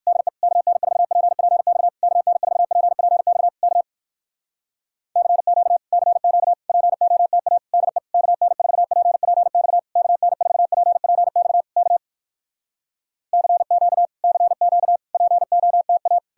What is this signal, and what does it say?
Morse code, heard as a keyed tone, about 59 wpm, 700 Hz. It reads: DE KN4PPX KN4PPX K CQ CQ POTA DE KN4PPX KN4PPX K CQ CQ POTA